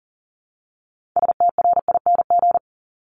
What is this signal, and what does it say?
Morse code, heard as a keyed tone, about 41 wpm, 700 Hz. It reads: STRING
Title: Multi-letter Phonemes Repeated 3x 40wpm